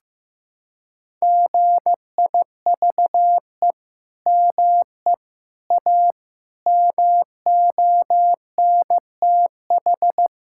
Morse code, heard as a keyed tone, about 15 wpm, 700 Hz. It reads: GIVE ME A MONTH
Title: Sentences from Top 1000 Words - Review Part 5 - 15wpm